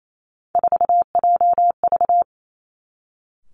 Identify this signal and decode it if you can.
Morse code, heard as a keyed tone, about 28 wpm, 700 Hz. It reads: 4JV
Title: Three Challenging Letter-Number Combinations 28wpm